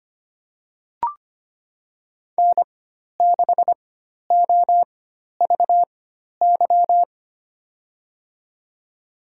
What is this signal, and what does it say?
Morse code, heard as a keyed tone, about 25 wpm, spaced as for 12 wpm, 700 Hz. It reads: N6OVY
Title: USA Call Signs - 5 Characters - 4x Character Spacing - 25wpm